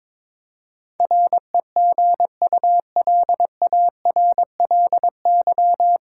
Morse code, heard as a keyed tone, about 22 wpm, 700 Hz. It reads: REGULARLY
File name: 9 Letter Words - First 3 Letters Given - 22wpm